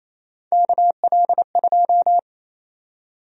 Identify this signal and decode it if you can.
Morse code, heard as a keyed tone, about 28 wpm, 700 Hz. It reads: KL2
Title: USA Call Sign Prefixes - 28wpm